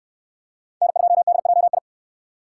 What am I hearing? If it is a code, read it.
Morse code, heard as a keyed tone, about 69 wpm, 700 Hz. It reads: N2DJI